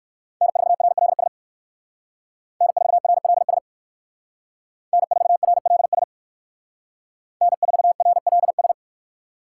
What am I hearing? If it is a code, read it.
Morse code, heard as a keyed tone, about 58 wpm, 700 Hz. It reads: N4RLS N4RLS N4RLS N4RLS